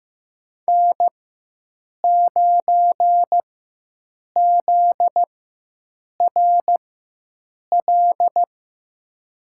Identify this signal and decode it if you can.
Morse code, heard as a keyed tone, about 15 wpm, spaced as for 7 wpm, 700 Hz. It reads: N9ZRL
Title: USA Call Signs - 5 Characters - 4x Character Spacing - 15wpm